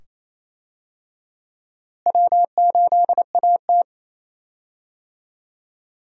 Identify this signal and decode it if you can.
Morse code, heard as a keyed tone, about 28 wpm, 700 Hz. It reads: W8AT
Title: USA Call Signs - 4 Characters - Repeated 2x - 28wpm